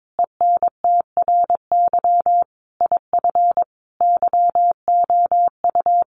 Morse code, heard as a keyed tone, about 22 wpm, 700 Hz. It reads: ENTRY IF YOU
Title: Sentences from Top 1500 Words 22wpm